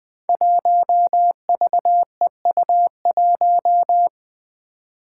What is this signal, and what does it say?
Morse code, heard as a keyed tone, about 20 wpm, 700 Hz. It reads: 1VEU1
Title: Five Challenging Letter-Number Combinations 20wpm